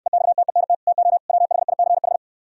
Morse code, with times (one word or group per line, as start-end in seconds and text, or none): 0.1-0.7 EXTENT
0.9-1.2 TO
1.3-2.2 WHICH